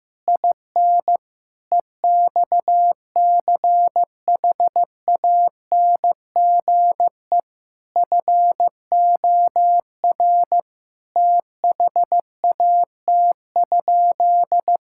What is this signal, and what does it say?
Morse code, heard as a keyed tone, about 15 wpm, 700 Hz. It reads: IN EXCHANGE FOR THAT?
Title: Sentences from Top 1000 Words - Review Part 6 - 15wpm